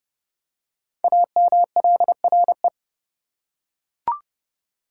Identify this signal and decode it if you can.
Morse code, heard as a keyed tone, about 30 wpm, 700 Hz. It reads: AMLRE
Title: Code Groups - 5 Letters - 30wpm